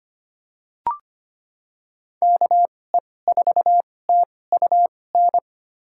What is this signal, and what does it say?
Morse code, heard as a keyed tone, about 25 wpm, spaced as for 18 wpm, 700 Hz. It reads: KE4TUN
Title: USA Call Signs - 6 Characters - 2x Character Spacing - 25wpm